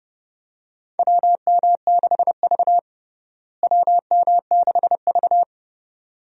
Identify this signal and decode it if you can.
Morse code, heard as a keyed tone, about 30 wpm, 700 Hz. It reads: WM6V WM6V
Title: USA Call Signs - 4 Characters - Repeated 2x - 30wpm